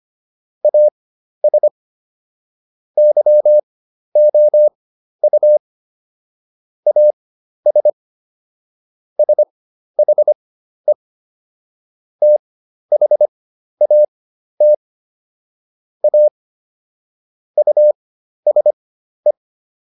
Morse code, heard as a keyed tone, about 25 wpm, spaced as for 12 wpm, 600 Hz. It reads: AS YOU AS SHE THAT A USE